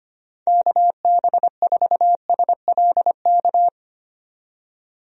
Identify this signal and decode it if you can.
Morse code, heard as a keyed tone, about 25 wpm, 700 Hz. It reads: KB4SLK
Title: USA Call Signs - 6 Characters - 25wpm